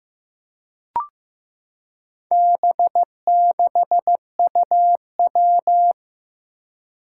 Morse code, heard as a keyed tone, about 15 wpm, 700 Hz. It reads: B6UW
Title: Four Challenging Letter-Number Combinations 15wpm